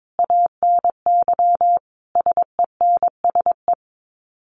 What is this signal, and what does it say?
Morse code, heard as a keyed tone, about 22 wpm, 700 Hz. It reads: ANY SENSE
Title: Sets of 5 Words Encore2 22wpm